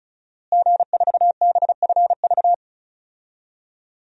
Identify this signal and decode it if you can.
Morse code, heard as a keyed tone, about 36 wpm, 700 Hz. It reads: G4BFV
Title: DXCC-100 Call Signs - 5 Characters - Repeated 2x - 35wpm